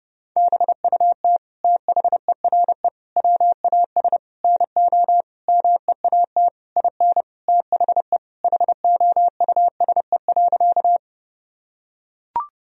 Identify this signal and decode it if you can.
Morse code, heard as a keyed tone, about 30 wpm, 700 Hz. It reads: BUT THERE WAS NO MEAT IN THE HOUSE.
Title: Sentences from Top 1000 Words - Review Part 3 - 30wpm